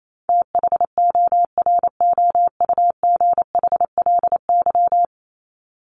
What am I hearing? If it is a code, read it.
Morse code, heard as a keyed tone, about 28 wpm, 700 Hz. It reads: THOROUGHLY
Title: LY Suffix - Top 100 Words with Base Words 28wpm